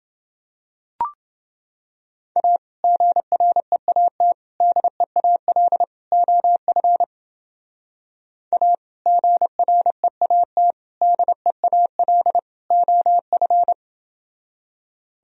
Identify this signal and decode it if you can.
Morse code, heard as a keyed tone, about 29 wpm, 700 Hz. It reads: A GREAT DEAL OF A GREAT DEAL OF